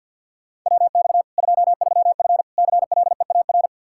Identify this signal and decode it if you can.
Morse code, heard as a keyed tone, about 52 wpm, 700 Hz. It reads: WX 23F CLEAR
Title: K1USN SST QSO Practice 50wpm